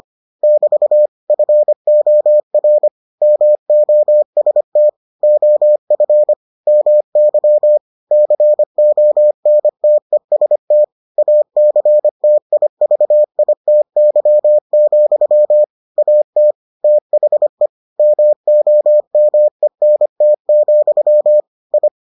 Morse code, heard as a keyed tone, about 25 wpm, 600 Hz. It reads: <BT> FOR MOST OF MY CONTEST ACTIVITY, AT THE MOMENT, I